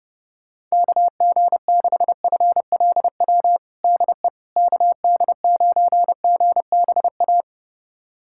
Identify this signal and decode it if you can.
Morse code, heard as a keyed tone, about 30 wpm, 700 Hz. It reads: KG6FLW DE KD9GBA
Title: USA Callsign DE Callsign 30wpm